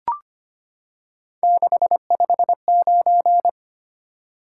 Morse code, heard as a keyed tone, about 25 wpm, 700 Hz. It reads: 659
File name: Three Letter-Number Combinations - 25wpm